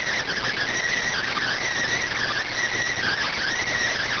Emergency Alert System (SAME) tones, heard as harsh buzzing bursts. It is heard from 0.8 to 3.7 s.